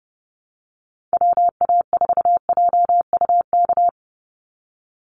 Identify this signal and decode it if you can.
Morse code, heard as a keyed tone, about 30 wpm, 700 Hz. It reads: WA4JUK
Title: Call Signs Repeated 3x - 30wpm